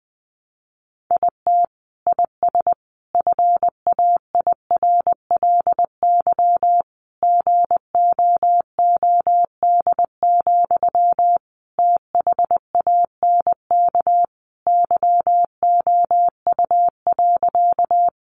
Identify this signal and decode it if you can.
Morse code, heard as a keyed tone, about 20 wpm, 700 Hz. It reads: IT IS FAIRLY GOOD, THANK YOU.
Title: Sentences from Top 1100 Words 20wpm